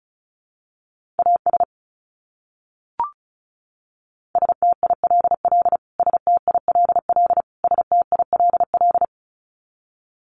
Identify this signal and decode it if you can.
Morse code, heard as a keyed tone, about 34 wpm, 700 Hz. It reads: AS STILL STILL STILL
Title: Top 300 Words Repeated 35wpm